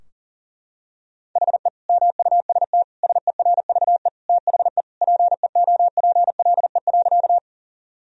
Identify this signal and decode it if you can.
Morse code, heard as a keyed tone, about 37 wpm, 700 Hz. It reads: HE MUST SERVE THE PEOPLE.
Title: Sentences from Top 1000 Words - Review Part 2 - 40wpm